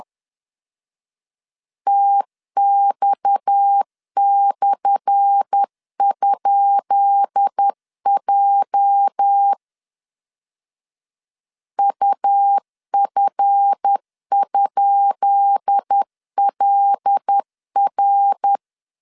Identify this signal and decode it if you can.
Morse code, heard as a keyed tone, about 10 wpm, 800 Hz. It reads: TX/?J UF?LR
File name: LEZIONI / 10 SIGNS / 10B INT 10wpm 2844